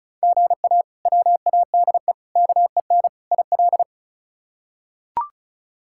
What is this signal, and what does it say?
Morse code, heard as a keyed tone, about 36 wpm, 700 Hz. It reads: GA WADE KEN IL
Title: K1USN SST QSO Practice - 35wpm